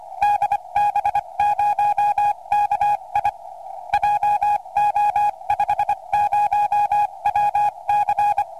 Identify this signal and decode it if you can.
Morse code, heard as a keyed tone, about 25 wpm, spaced as for 22 wpm, 830 Hz. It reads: DB0KI JO50WC
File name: ok1teh / db0ki 432